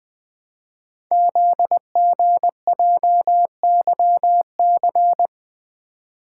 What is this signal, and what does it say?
Morse code, heard as a keyed tone, about 20 wpm, 700 Hz. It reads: ZGJYC
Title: Code Groups - 5 Letter:Numbers - 20wpm